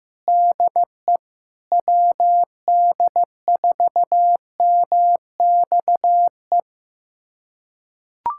Morse code, heard as a keyed tone, about 15 wpm, 700 Hz. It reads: DE WD4MXE
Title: USA Callsign DE Callsign 15wpm